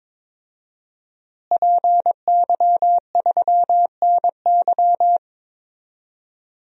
Morse code, heard as a keyed tone, about 22 wpm, 700 Hz. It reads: PY3NY
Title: DXCC-100 Call Signs - 5 Characters - Repeated 2x - 22wpm